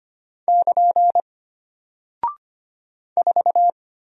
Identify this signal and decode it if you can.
Morse code, heard as a keyed tone, about 25 wpm, 700 Hz. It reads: <KN> 4
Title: Single Character (Letter:Number:Punctuation:Prosign) Alt 25wpm